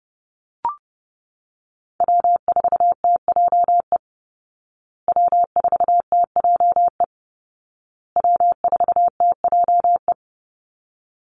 Morse code, heard as a keyed tone, about 30 wpm, 700 Hz. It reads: W4TJE W4TJE W4TJE
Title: Call Signs Repeated 2x - 30wpm